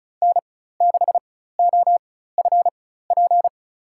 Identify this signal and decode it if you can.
Morse code, heard as a keyed tone, about 34 wpm, spaced as for 16 wpm, 700 Hz. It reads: N6OFP
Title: USA Call Signs - 5 Characters - 4x Character Spacing - 35wpm